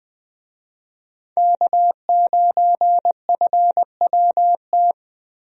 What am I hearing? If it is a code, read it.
Morse code, heard as a keyed tone, about 21 wpm, 700 Hz. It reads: K9FWT
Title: USA Call Signs - 5 Characters - Repeated 3x - 20wpm